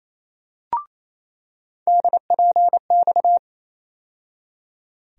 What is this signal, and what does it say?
Morse code, heard as a keyed tone, about 28 wpm, 700 Hz. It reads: DPX
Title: USA Call Sign Postfixes - 3 Letters 28wpm